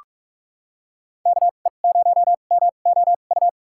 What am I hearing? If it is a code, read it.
Morse code, heard as a keyed tone, about 46 wpm, spaced as for 33 wpm, 700 Hz. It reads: KE0MOU
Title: USA Call Signs - 2x Character Spacing - 45wpm